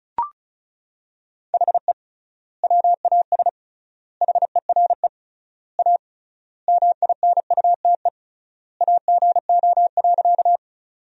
Morse code, heard as a keyed tone, about 34 wpm, 700 Hz. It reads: HE WAS HERE A MINUTE AGO.
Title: 3x Word-Spacing - Sentences from Top 500 Words - 35wpm